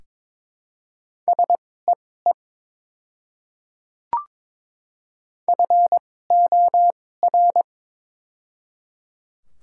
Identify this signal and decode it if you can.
Morse code, heard as a keyed tone, about 22 wpm, spaced as for 16 wpm, 700 Hz. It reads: SEE FOR
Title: 2x Character Spacing - Top 100 Words - 22wpm